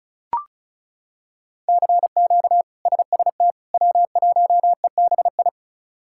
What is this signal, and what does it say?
Morse code, heard as a keyed tone, about 34 wpm, 700 Hz. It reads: CQ SST W1EBI
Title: K1USN SST QSO Practice - 35wpm